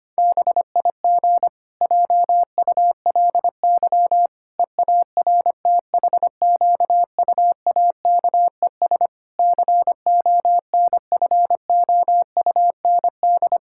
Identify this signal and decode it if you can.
Morse code, heard as a keyed tone, about 25 wpm, 700 Hz. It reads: BIG JULY EARTHQUAKES CONFOUND
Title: Pangram Challenge v2 - 25wpm